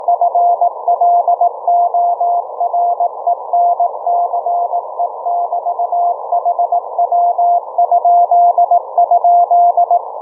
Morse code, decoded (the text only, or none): FLORENCE<BT>HW??